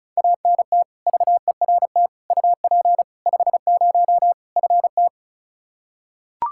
Morse code, heard as a keyed tone, about 34 wpm, 700 Hz. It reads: ANT VERT UP 50 FT
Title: Practice QSO - Exchange 2 - Info - 35wpm